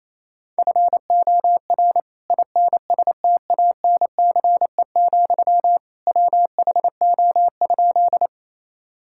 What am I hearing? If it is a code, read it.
Morse code, heard as a keyed tone, about 28 wpm, 700 Hz. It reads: FOR INSTANCE, WHO?
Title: Sentences from Top 1000 Words - Review Part 5 - 28wpm